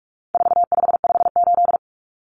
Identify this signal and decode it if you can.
Morse code, heard as a keyed tone, about 44 wpm, 700 Hz. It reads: 4558
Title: Numbers - 4-Digit 45wpm